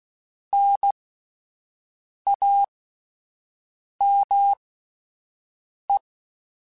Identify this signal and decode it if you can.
Morse code, heard as a keyed tone, about 16 wpm, spaced as for 5 wpm, 800 Hz. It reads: NAME
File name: (spaced) / QSOs Spaced-mp3s / WA7QOX 2 @10 sp